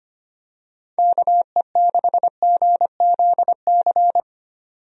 Morse code, heard as a keyed tone, about 25 wpm, 700 Hz. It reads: KE6GZC
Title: USA Call Signs - 6 Characters - 25wpm